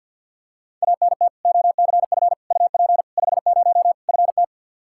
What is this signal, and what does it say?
Morse code, heard as a keyed tone, about 47 wpm, 700 Hz. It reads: ANT OCF UP 50 FT
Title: Practice QSO - Exchange 2 - Info - 50wpm